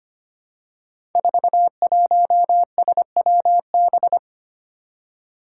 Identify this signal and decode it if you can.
Morse code, heard as a keyed tone, about 25 wpm, 700 Hz. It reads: 41SWB
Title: Five Challenging Letter-Number Combinations 25wpm